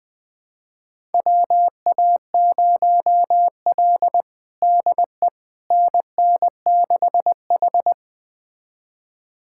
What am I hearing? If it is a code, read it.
Morse code, heard as a keyed tone, about 21 wpm, 700 Hz. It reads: WA0L DE NN6H
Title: USA Callsign DE Callsign - 4 Characters - 20wpm